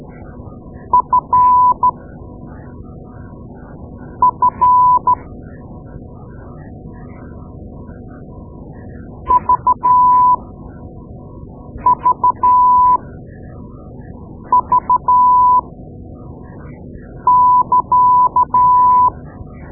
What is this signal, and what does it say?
Morse code, heard as a keyed tone, about 12 wpm, spaced as for 4 wpm, 1000 Hz. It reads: FF VVV<KA>
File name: Morse Lessons / Morse Lessons / Tape-2A